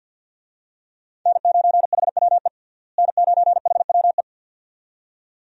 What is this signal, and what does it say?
Morse code, heard as a keyed tone, about 49 wpm, 700 Hz. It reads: N9HWE N9HWE
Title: USA Call Signs - 5 Characters - Repeated 2x - 50wpm